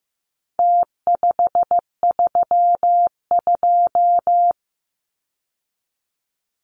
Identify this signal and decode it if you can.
Morse code, heard as a keyed tone, about 15 wpm, 700 Hz. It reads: T532
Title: Numbers - Contest Style - 4-Digit 15wpm